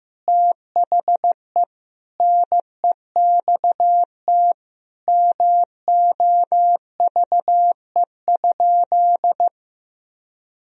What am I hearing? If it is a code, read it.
Morse code, heard as a keyed tone, about 16 wpm, 700 Hz. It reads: THE NEXT MOVE?
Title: Sentences from Top 1000 Words - Review Part 1 - 15wpm